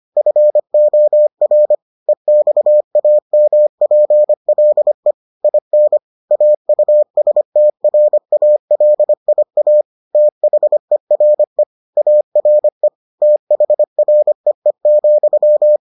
Morse code, heard as a keyed tone, about 25 wpm, 600 Hz. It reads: FOR EXAMPLE IN AUSTRALIA THERE ARE THREE,